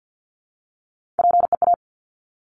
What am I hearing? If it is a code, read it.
Morse code, heard as a keyed tone, about 50 wpm, 700 Hz. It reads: PEA
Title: Top Three Letter Combinations in Words 50wpm